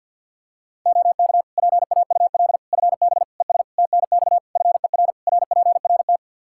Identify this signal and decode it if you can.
Morse code, heard as a keyed tone, about 50 wpm, 700 Hz. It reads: OK PAUL FB ES TNX FER RPRT
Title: Practice QSO - Exchange 2 - Info - 50wpm